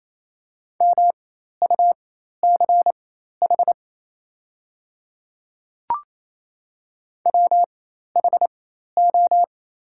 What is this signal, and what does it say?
Morse code, heard as a keyed tone, about 28 wpm, spaced as for 13 wpm, 700 Hz. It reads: MUCH WHO
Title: 4x Character Spacing - Top 300 Words - 28wpm